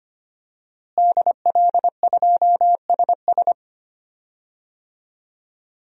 Morse code, heard as a keyed tone, about 25 wpm, 700 Hz. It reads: DL2SS
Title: DXCC-100 Call Signs - 5 Characters - Repeated 3x - 25wpm